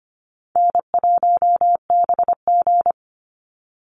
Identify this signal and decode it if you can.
Morse code, heard as a keyed tone, about 25 wpm, 700 Hz. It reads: N1BG
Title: Easy Call Signs Repeated 2x - 25wpm